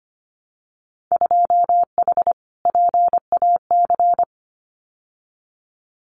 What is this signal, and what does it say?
Morse code, heard as a keyed tone, about 25 wpm, 700 Hz. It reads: 2H PAC
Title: Winter Field Day Exchanges Encore 25wpm